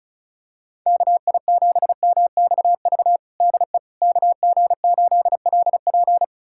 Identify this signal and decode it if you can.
Morse code, heard as a keyed tone, about 36 wpm, 700 Hz. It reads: KI7MXV DE KG8LP
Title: USA Callsign DE Callsign 35wpm